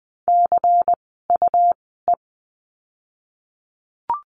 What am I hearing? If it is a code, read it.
Morse code, heard as a keyed tone, about 20 wpm, spaced as for 14 wpm, 700 Hz. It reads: CUE
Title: 101 Lesson 17f - Letter u - Words 20wpm